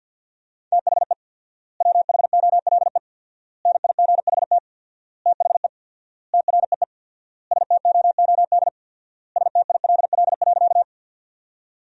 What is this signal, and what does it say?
Morse code, heard as a keyed tone, about 47 wpm, 700 Hz. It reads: THE WHOLE NIGHT THE TREE STOOD STILL.
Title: 4x Word-Spacing - Sentences from Top 500 Words - 50wpm